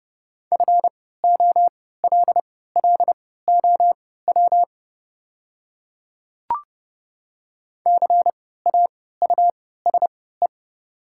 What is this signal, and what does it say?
Morse code, heard as a keyed tone, about 29 wpm, spaced as for 17 wpm, 690 Hz. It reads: FOLLOW CAUSE
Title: 3x Character Spacing - Top 300 Words - 30wpm